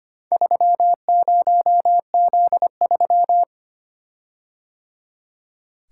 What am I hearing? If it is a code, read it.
Morse code, heard as a keyed tone, about 25 wpm, 700 Hz. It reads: 30Z3
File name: practice / 25wpm / Four Letter-Number Combinations - 25wpm